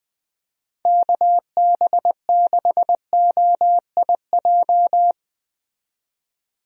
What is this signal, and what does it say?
Morse code, heard as a keyed tone, about 20 wpm, 700 Hz. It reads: KB6OIJ
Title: USA Call Signs - 6 Characters - 20wpm